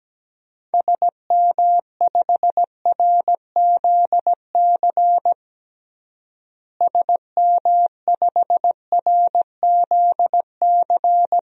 Morse code, heard as a keyed tone, about 17 wpm, 700 Hz. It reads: SM5RZC SM5RZC
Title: DXCC-100 Call Signs - 6 Characters - Repeated 2x - 17wpm